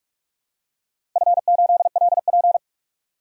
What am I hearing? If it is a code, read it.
Morse code, heard as a keyed tone, about 46 wpm, 700 Hz. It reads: F8LP
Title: DXCC-100 Call Signs - 4 Characters - Repeated 3x - 45wpm